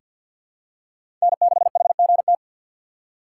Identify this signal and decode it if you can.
Morse code, heard as a keyed tone, about 48 wpm, 690 Hz. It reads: N6HGT
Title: USA Call Signs - 6x Character Spacing - 50wpm